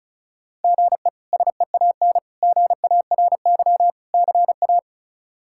Morse code, heard as a keyed tone, about 36 wpm, 700 Hz. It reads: GE SEAN GARY CA
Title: K1USN SST QSO Practice - 35wpm